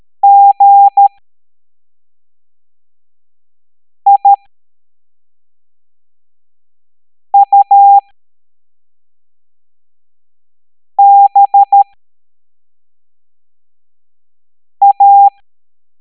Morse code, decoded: GIUBA